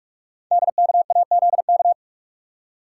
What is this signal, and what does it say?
Morse code, heard as a keyed tone, about 44 wpm, 700 Hz. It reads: DKAZK
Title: Code Groups - 5 Letters - 45wpm